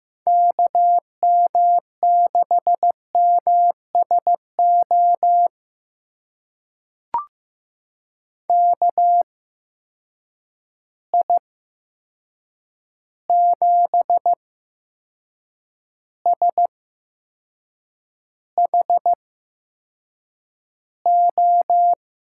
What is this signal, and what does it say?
Morse code, heard as a keyed tone, about 15 wpm, spaced as for 6 wpm, 700 Hz. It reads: KM6MSO KI7SHO
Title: USA Call Signs - 6 Characters - 8x Character Spacing - 15wpm